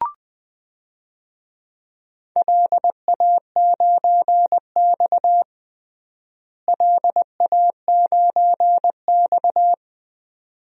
Morse code, heard as a keyed tone, about 20 wpm, 700 Hz. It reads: LA9X LA9X
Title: DXCC-100 Call Signs - 4 Characters - Repeated 2x - 20wpm